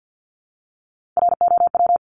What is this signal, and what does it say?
Morse code, heard as a keyed tone, about 50 wpm, 700 Hz. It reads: ROW
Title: Top 1751-2000 Words 50wpm